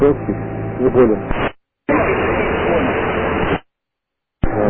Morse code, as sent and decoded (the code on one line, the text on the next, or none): none